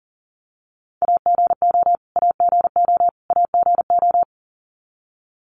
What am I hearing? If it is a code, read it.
Morse code, heard as a keyed tone, about 39 wpm, 700 Hz. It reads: AGO AGO AGO
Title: Top 500 Words Repeated 40wpm